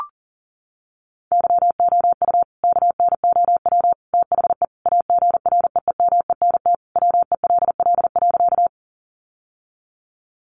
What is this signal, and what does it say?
Morse code, heard as a keyed tone, about 40 wpm, 700 Hz. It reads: YOU KNOW THE AGREEMENT WELL.
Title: Sentences from Top 1500 Words 40wpm